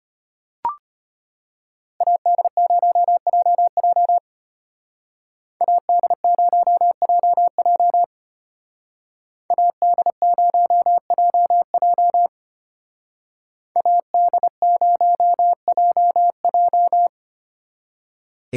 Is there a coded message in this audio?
Yes: Morse code from 2.0 to 17.1 s, heard as a keyed tone, about 33 wpm, 700 Hz. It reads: AD0JJ AD0JJ AD0JJ AD0JJ